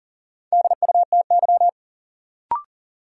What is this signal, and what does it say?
Morse code, heard as a keyed tone, about 40 wpm, 700 Hz. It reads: DUTY
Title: Sentences from Top 1500 Words - New Word Drill Alt - 40wpm